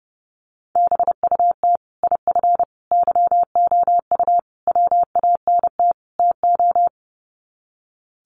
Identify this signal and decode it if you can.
Morse code, heard as a keyed tone, about 30 wpm, 700 Hz. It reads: BUT IF YOU WANT TO
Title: Sets of 5 Words Encore2 30wpm